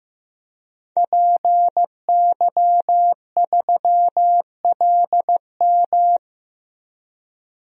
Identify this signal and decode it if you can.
Morse code, heard as a keyed tone, about 15 wpm, 700 Hz. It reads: PY3LM
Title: DXCC-100 Call Signs - 5 Characters - Repeated 3x - 15wpm